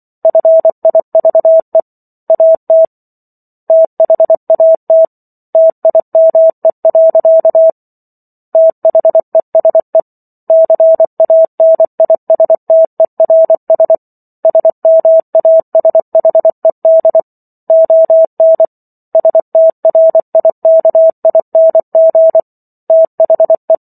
Morse code, heard as a keyed tone, about 24 wpm, 650 Hz. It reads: FIVE AT THAT TIME. THESE CANISTERS SMASHED ON STRIKING THE
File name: Chapter 15 - 24 WPM